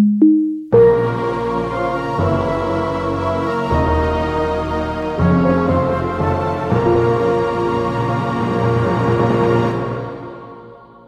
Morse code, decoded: none

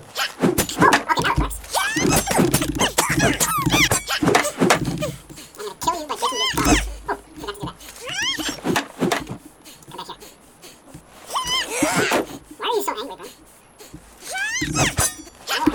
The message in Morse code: none

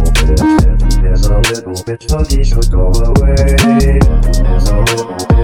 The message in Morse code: none

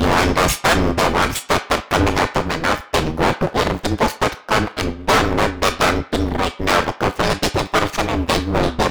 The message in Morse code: none

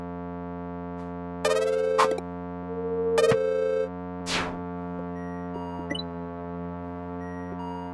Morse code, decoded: none

